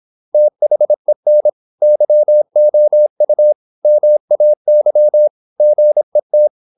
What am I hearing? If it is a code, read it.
Morse code, heard as a keyed tone, about 26 wpm, 600 Hz. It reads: THEN YOU MAY GET